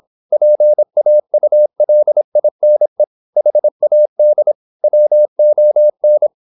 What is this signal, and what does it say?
Morse code, heard as a keyed tone, about 26 wpm, 600 Hz. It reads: PAULINE HAD WON
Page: Never 26wpm | CW med Gnister